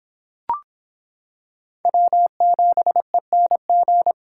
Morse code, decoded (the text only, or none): W7ENG